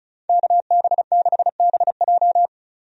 Morse code, heard as a keyed tone, about 34 wpm, 700 Hz. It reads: KB6BJ
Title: USA Call Signs - 5 Characters - 35wpm